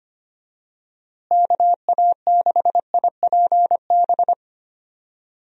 Morse code, heard as a keyed tone, about 25 wpm, 700 Hz. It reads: KA6IPB
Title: USA Call Signs - 6 Characters - Repeated 2x - 25wpm